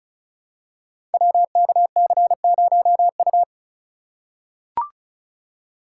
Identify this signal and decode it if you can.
Morse code, heard as a keyed tone, about 36 wpm, 700 Hz. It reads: WKC0U
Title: Code Groups - 5 Letter:Numbers - 35wpm